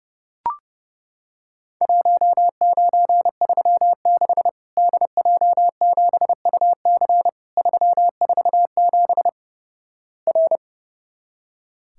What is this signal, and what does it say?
Morse code, heard as a keyed tone, about 30 wpm, 700 Hz. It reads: 1936 DJ7UC 347 R
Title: Worked All Europe DX Contest - QTC Exchange - 30wpm